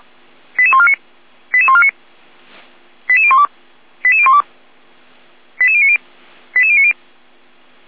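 Five-tone selective call (ZVEI), heard as quick runs of five tones, tones of about 70 ms. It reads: 88149 88149 88012 88012 88090 88090
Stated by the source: ZVEI five-tone selective call 88150 received on SDR.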